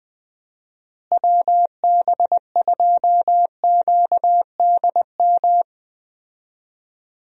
Morse code, heard as a keyed tone, about 20 wpm, 700 Hz. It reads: WB2QDM
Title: USA Call Signs - 6 Characters - Repeated 3x - 20wpm